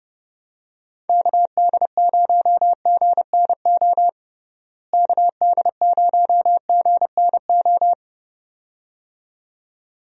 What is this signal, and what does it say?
Morse code, heard as a keyed tone, about 30 wpm, 700 Hz. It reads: KD0GNO KD0GNO
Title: USA Call Signs - 6 Characters - Repeated 2x - 30wpm